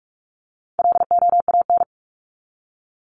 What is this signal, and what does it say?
Morse code, heard as a keyed tone, about 44 wpm, 700 Hz. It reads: LOAN
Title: Top 1501-1750 Words 45wpm